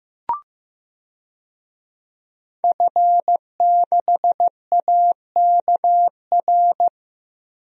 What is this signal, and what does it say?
Morse code, heard as a keyed tone, about 15 wpm, 700 Hz. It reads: F6AKR
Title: DXCC-100 Call Signs - 5 Characters - Repeated 2x - 15wpm